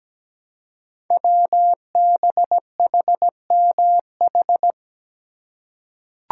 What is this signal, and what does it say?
Morse code, heard as a keyed tone, about 17 wpm, 700 Hz. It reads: WBHMH
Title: Code Groups - 5 Letter:Numbers - 17wpm